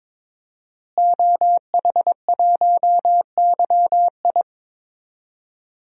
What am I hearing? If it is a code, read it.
Morse code, heard as a keyed tone, about 22 wpm, 700 Hz. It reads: OH1YI
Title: DXCC-100 Call Signs - 5 Characters - Repeated 2x - 22wpm